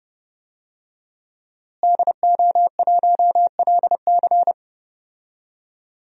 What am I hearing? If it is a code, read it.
Morse code, heard as a keyed tone, about 30 wpm, 700 Hz. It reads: DO1LC
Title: DXCC-100 Call Signs - 5 Characters - Repeated 2x - 30wpm